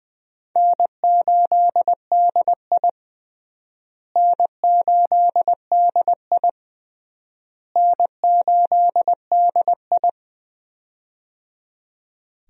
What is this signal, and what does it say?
Morse code, heard as a keyed tone, about 20 wpm, 700 Hz. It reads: N8DI N8DI N8DI
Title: USA Call Signs - 4 Characters - Repeated 3x - 20wpm